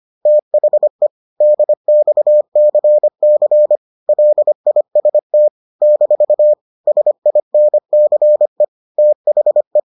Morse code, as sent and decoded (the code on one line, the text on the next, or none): - .... / -.. -..- -.-. -.-. / .-.. .. ... - / -....- / ... .. -. -.-. / - .... .
THE DXCC LIST - SINCE THE